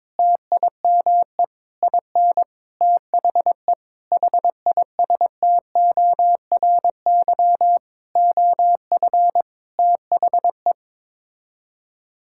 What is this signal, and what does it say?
Morse code, heard as a keyed tone, about 22 wpm, 700 Hz. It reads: TIME IN THE HISTORY OF THE
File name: Sets of 9 Words - 22wpm